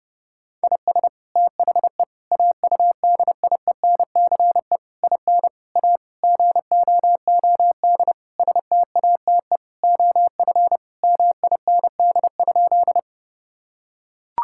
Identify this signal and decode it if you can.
Morse code, heard as a keyed tone, about 30 wpm, 700 Hz. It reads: IS THE AUDIENCE IN A GOOD STATE OF MIND?
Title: Sentences from Top 1000 Words - Review Part 5 - 30wpm